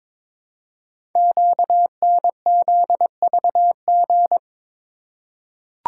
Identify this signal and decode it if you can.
Morse code, heard as a keyed tone, about 22 wpm, 690 Hz. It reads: QNZVG
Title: Code Groups - 5 Letters - 22wpm